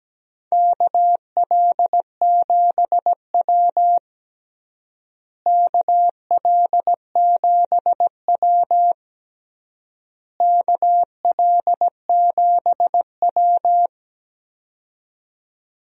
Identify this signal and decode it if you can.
Morse code, heard as a keyed tone, about 17 wpm, 700 Hz. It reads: KL7W KL7W KL7W
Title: DXCC-100 Call Signs - 4 Characters - Repeated 3x - 17wpm